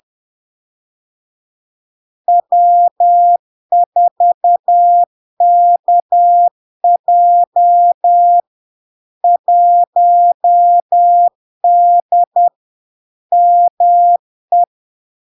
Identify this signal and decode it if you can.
Morse code, heard as a keyed tone, about 10 wpm, 700 Hz. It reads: W4KJ 1D ME
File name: LARK-FD-2022-Log-in-CW-10wpm-700-Hz